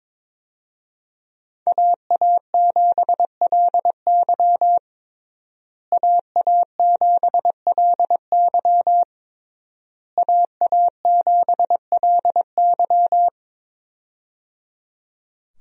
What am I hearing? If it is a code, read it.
Morse code, heard as a keyed tone, about 22 wpm, 690 Hz. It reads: AA7LY AA7LY AA7LY
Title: USA Call Signs - 5 Characters - Repeated 3x - 22wpm